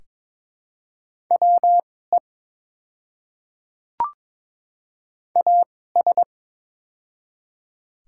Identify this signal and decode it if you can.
Morse code, heard as a keyed tone, about 22 wpm, spaced as for 16 wpm, 700 Hz. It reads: WE AS
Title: 2x Character Spacing - Top 100 Words - 22wpm